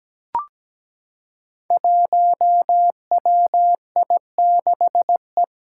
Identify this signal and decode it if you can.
Morse code, heard as a keyed tone, about 17 wpm, 700 Hz. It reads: 1WI6E
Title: Five Challenging Letter-Number Combinations 17wpm